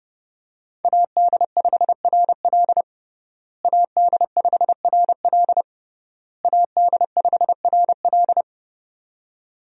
Morse code, heard as a keyed tone, about 30 wpm, 700 Hz. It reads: AD5RL AD5RL AD5RL
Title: USA Call Signs - 5 Characters - Repeated 3x - 30wpm